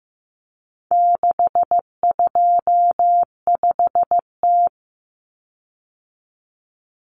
Morse code, heard as a keyed tone, about 15 wpm, 700 Hz. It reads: 625T
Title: Common Cut Numbers - 4-Digit 15wpm